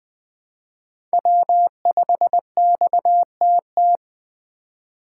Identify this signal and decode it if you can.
Morse code, heard as a keyed tone, about 20 wpm, 700 Hz. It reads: W5XTT
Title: USA Call Signs - 3x Character Spacing - 20wpm